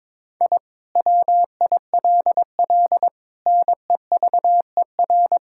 Morse code, heard as a keyed tone, about 22 wpm, 700 Hz. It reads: I WILL NEVER
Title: Sentences from Top 1000 Words - Review Part 6 - 22wpm